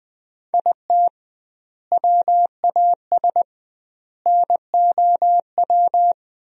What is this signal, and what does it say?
Morse code, heard as a keyed tone, about 20 wpm, 700 Hz. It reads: IT WAS NOW
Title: 2x Word-Spacing - Sentences from Top 200 Words - 20wpm